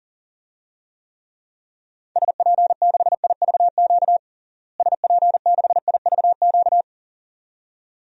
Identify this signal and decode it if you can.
Morse code, heard as a keyed tone, about 40 wpm, 700 Hz. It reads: SP6IVQ SP6IVQ
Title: DXCC-100 Call Signs - 6 Characters - Repeated 2x - 40wpm